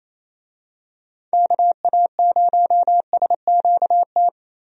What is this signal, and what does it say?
Morse code, heard as a keyed tone, about 27 wpm, 700 Hz. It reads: KA0SQT
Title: USA Call Signs - 6 Characters - 28wpm